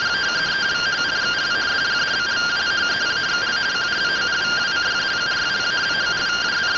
Two-tone teleprinter signal, about 170 Hz shift.